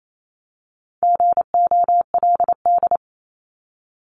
Morse code, heard as a keyed tone, about 28 wpm, 700 Hz. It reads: GOLD
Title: Top 500 Words 28wpm